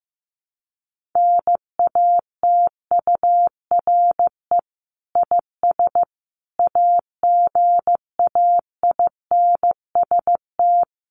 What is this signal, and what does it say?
Morse code, heard as a keyed tone, about 15 wpm, 700 Hz. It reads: NATURE IS AGAINST
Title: Sentences from Top 1500 Words 15wpm